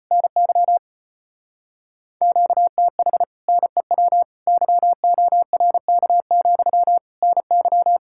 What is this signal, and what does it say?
Morse code, heard as a keyed tone, about 33 wpm, 700 Hz. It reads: NY QTH NEW YORK, NY